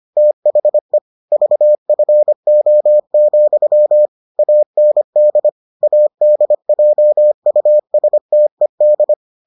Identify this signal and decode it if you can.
Morse code, heard as a keyed tone, about 25 wpm, 600 Hz. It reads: THE VFO, AND ADJUSTED